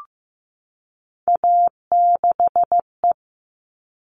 Morse code, heard as a keyed tone, about 15 wpm, 700 Hz. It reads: A6E
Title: All Cut Numbers - 3-Digit 15wpm